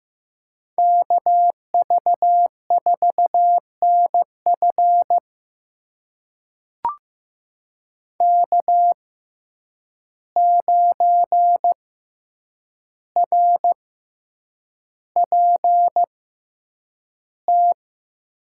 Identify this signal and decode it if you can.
Morse code, heard as a keyed tone, about 15 wpm, spaced as for 8 wpm, 700 Hz. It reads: KV4NF K9RPT
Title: USA Call Signs - 5 Characters - 6x Character Spacing - 15wpm